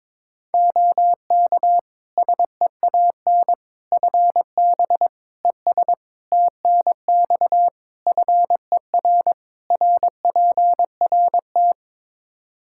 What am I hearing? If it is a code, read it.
Morse code, heard as a keyed tone, about 22 wpm, 700 Hz. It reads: OK SEAN FB ES TNX FER RPRT
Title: Practice QSO - Exchange 2 - Info - 22wpm